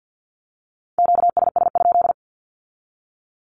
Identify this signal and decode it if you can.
Morse code, heard as a keyed tone, about 50 wpm, 700 Hz. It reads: QSS?
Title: Q-Codes with Spoken Meaning - 50wpm